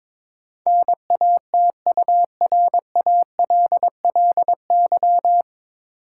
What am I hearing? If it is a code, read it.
Morse code, heard as a keyed tone, about 22 wpm, 700 Hz. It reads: NATURALLY
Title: Sentences from Top 1300 Words - New Word Drill - 22wpm